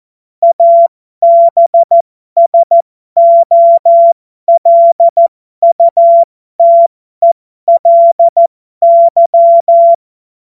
Morse code, heard as a keyed tone, about 14 wpm, spaced as for 12 wpm, 680 Hz. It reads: ABSOLUTELY
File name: Morsekurs / Witze